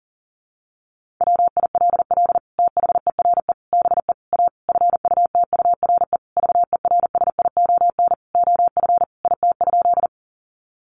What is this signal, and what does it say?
Morse code, heard as a keyed tone, about 40 wpm, 700 Hz. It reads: WILL THERE BE A FUTURE VERSION OF IT?
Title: Sentences from Top 1400 Words 40wpm